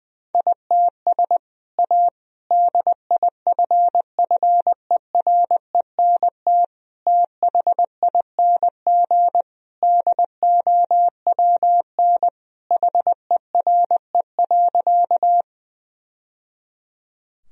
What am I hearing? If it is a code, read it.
Morse code, heard as a keyed tone, about 20 wpm, 700 Hz. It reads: ITS A DIFFERENT THING DOWN HERE.
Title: Sentences from Top 1000 Words - Review Part 3 - 20wpm